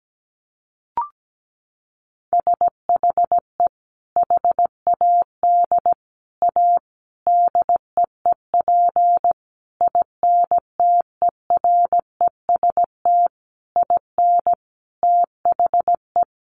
Morse code, heard as a keyed tone, about 17 wpm, 700 Hz. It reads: SHE HAD A DEEP INTEREST IN THE
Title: Sentences from Top 1300 Words 17wpm